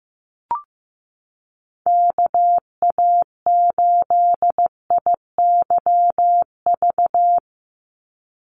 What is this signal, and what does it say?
Morse code, heard as a keyed tone, about 15 wpm, 700 Hz. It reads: KA8IYV
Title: USA Call Signs 15wpm